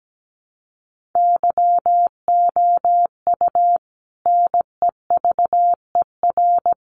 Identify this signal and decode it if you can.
Morse code, heard as a keyed tone, about 17 wpm, 700 Hz. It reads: YOU NEVER
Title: Sentences from Top 1500 Words 17wpm